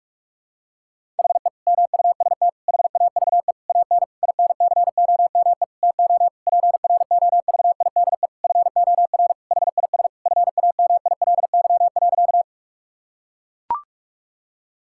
Morse code, heard as a keyed tone, about 41 wpm, 700 Hz. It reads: HE MUST HAVE AN INCOME TO PROVIDE FOR HIS FAMILY.
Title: Sentences from Top 1000 Words - Review Part 5 - 45wpm